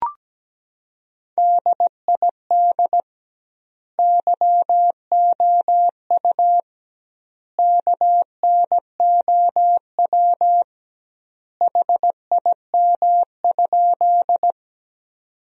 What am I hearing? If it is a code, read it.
Morse code, heard as a keyed tone, about 17 wpm, 700 Hz. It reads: DID YOU KNOW HIM?
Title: 2x Word-Spacing - Sentences from Top 100 Words - 17wpm